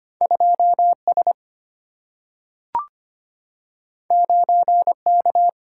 Morse code, heard as a keyed tone, about 25 wpm, 700 Hz. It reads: 2S 9K
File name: Two Letter-Number Combinations - 25wpm